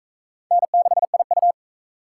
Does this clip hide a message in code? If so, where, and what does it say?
Yes: Morse code from 0.5 to 1.5 s, heard as a keyed tone, about 37 wpm, 700 Hz. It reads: N6IU